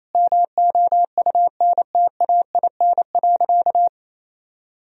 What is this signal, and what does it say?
Morse code, heard as a keyed tone, about 28 wpm, 700 Hz. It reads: MOUNTAIN.
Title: Sentences from Top 1000 Words - Review Part 3 - 28wpm